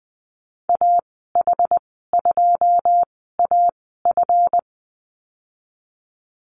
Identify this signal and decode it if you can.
Morse code, heard as a keyed tone, about 20 wpm, spaced as for 15 wpm, 700 Hz. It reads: AH2AF
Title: 186 Lesson 30g - Letter b - Call Signs 20wpm